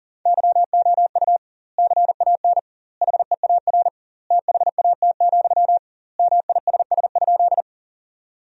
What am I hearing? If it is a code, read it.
Morse code, heard as a keyed tone, about 37 wpm, 700 Hz. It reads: YOU CAN HEAR THAT, MISS?
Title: 2x Word-Spacing - Sentences from Top 500 Words - 40wpm